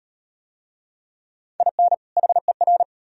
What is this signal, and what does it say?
Morse code, heard as a keyed tone, about 38 wpm, 700 Hz. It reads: IN HER